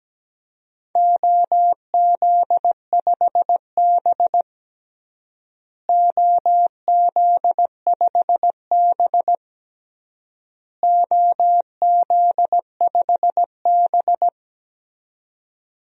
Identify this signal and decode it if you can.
Morse code, heard as a keyed tone, about 17 wpm, 700 Hz. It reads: OZ5B OZ5B OZ5B
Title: DXCC-100 Call Signs - 4 Characters - Repeated 3x - 17wpm